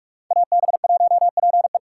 Morse code, heard as a keyed tone, about 46 wpm, 700 Hz. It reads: AB1PE